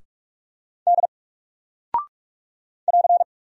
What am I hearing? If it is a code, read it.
Morse code, heard as a keyed tone, about 46 wpm, 700 Hz. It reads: D <AR>
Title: Single Character (Letter:Number:Punctuation:Prosign) 45wpm